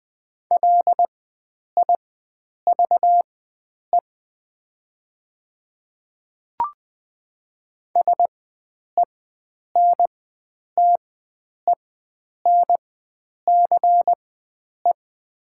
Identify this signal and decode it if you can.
Morse code, heard as a keyed tone, about 20 wpm, spaced as for 9 wpm, 690 Hz. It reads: LIVE SENTENCE
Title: 4x Character Spacing - Top 300 Words - 20wpm